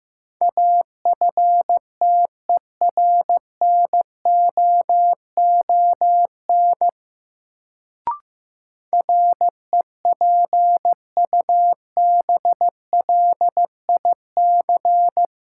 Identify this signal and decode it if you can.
Morse code, heard as a keyed tone, about 15 wpm, 700 Hz. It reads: AFTERNOON REPUBLIC
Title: Sentences from Top 1500 Words - New Word Drill - 15wpm